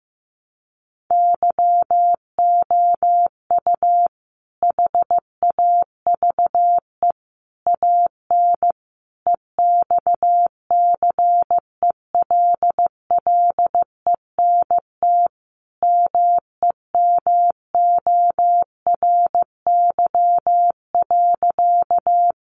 Morse code, heard as a keyed tone, about 15 wpm, 700 Hz. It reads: YOU HAVE AN EXCELLENT MEMORY.
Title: Sentences from Top 1300 Words 15wpm